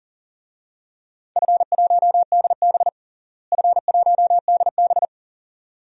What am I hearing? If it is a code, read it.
Morse code, heard as a keyed tone, about 40 wpm, 700 Hz. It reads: F1DB F1DB
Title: DXCC-100 Call Signs - 4 Characters - Repeated 2x - 40wpm